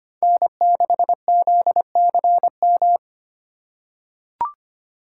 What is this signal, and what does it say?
Morse code, heard as a keyed tone, about 25 wpm, 700 Hz. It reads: N6ZCM
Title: USA Call Signs - 5 Characters - 4x Character Spacing - 25wpm